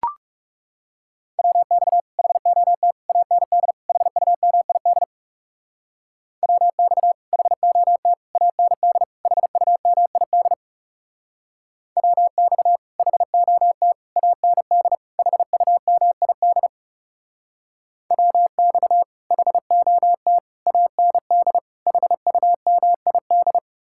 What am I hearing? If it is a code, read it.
Morse code, heard as a keyed tone, about 40 wpm, 700 Hz. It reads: WX HOT AND HUMID WX HOT AND HUMID WX HOT AND HUMID WX HOT AND HUMID